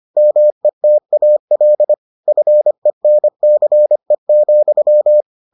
Morse code, heard as a keyed tone, about 25 wpm, 600 Hz. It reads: METAL FENCE,